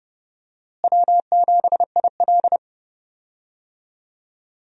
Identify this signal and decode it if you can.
Morse code, heard as a keyed tone, about 30 wpm, 700 Hz. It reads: W7IL
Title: USA Call Signs - 4 Characters - Repeated 2x - 30wpm